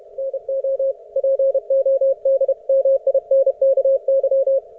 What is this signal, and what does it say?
Morse code, heard as a keyed tone, about 31 wpm, 540 Hz. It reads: NO PODMINKY